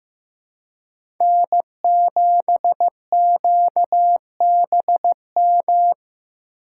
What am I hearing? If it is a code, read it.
Morse code, heard as a keyed tone, about 15 wpm, 700 Hz. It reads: N7QBM
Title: USA Call Signs - 5 Characters - 4x Character Spacing - 15wpm